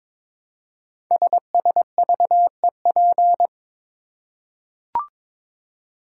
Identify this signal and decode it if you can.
Morse code, heard as a keyed tone, about 22 wpm, 700 Hz. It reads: SSVEP
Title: Code Groups - 5 Letters - 22wpm